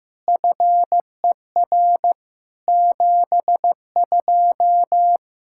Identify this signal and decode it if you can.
Morse code, heard as a keyed tone, about 16 wpm, 700 Hz. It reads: FER 72
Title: Practice QSO - Exchange 3 - More Info - 15wpm